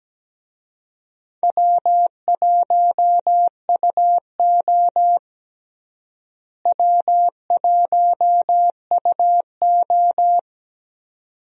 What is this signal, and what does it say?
Morse code, heard as a keyed tone, about 17 wpm, 700 Hz. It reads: W1UO W1UO
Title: USA Call Signs - 4 Characters - Repeated 2x - 17wpm